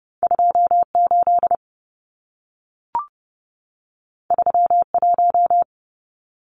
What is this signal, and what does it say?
Morse code, heard as a keyed tone, about 30 wpm, 700 Hz. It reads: 28 31
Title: Numbers - 2-Digit 30wpm